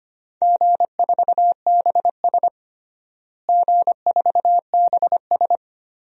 Morse code, heard as a keyed tone, about 25 wpm, 700 Hz. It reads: G4BS G4BS
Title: DXCC-100 Call Signs - 4 Characters - Repeated 2x - 25wpm